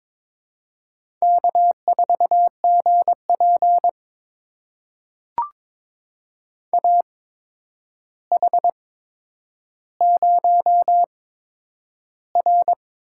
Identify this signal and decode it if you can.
Morse code, heard as a keyed tone, about 22 wpm, spaced as for 9 wpm, 700 Hz. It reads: K4GP AH0R
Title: USA Call Signs - 8x Character Spacing - 22wpm